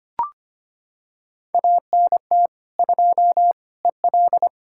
Morse code, heard as a keyed tone, about 25 wpm, 700 Hz. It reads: ANT 2 EL
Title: Practice QSO - Exchange 2 - Info - 25wpm